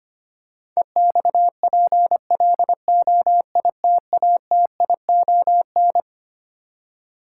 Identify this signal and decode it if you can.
Morse code, heard as a keyed tone, about 25 wpm, 700 Hz. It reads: EXPLOITATION
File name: practice / 25wpm / Twelve Letter Words - 25wpm